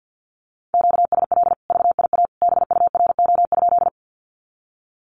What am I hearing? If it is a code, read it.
Morse code, heard as a keyed tone, about 50 wpm, 700 Hz. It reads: QSL VIA BURO?
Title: Common QSO Phrases 50wpm